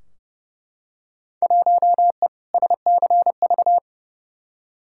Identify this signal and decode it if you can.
Morse code, heard as a keyed tone, about 31 wpm, 700 Hz. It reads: 1E SCV
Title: Field Day Exchanges - 30wpm